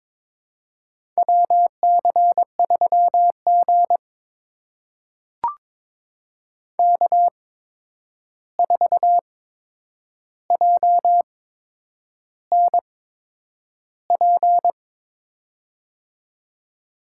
Morse code, heard as a keyed tone, about 22 wpm, spaced as for 9 wpm, 700 Hz. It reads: WC3G K4JNP
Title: USA Call Signs - 8x Character Spacing - 22wpm